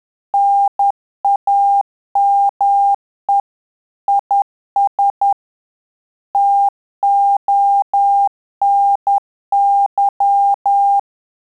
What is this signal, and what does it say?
Morse code, heard as a keyed tone, about 10 wpm, 800 Hz. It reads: NAME IS TONY
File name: Telegraphy-group / QSO CW 50 CPM